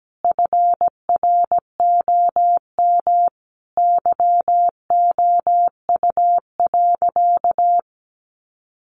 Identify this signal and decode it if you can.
Morse code, heard as a keyed tone, about 17 wpm, 700 Hz. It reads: FROM YOU.
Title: Sentences from Top 100 Words 17wpm